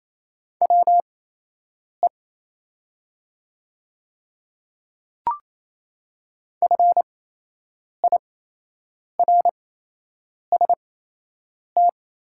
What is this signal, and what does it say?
Morse code, heard as a keyed tone, about 27 wpm, spaced as for 8 wpm, 700 Hz. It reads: WE FIRST
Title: 8x Character Spacing - Top 200 Words - 28wpm